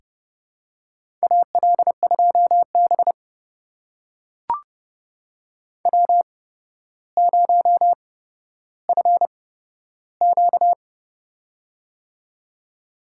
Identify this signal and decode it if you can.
Morse code, heard as a keyed tone, about 30 wpm, spaced as for 13 wpm, 700 Hz. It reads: AL2B W0FQ
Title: USA Call Signs - 8x Character Spacing - 30wpm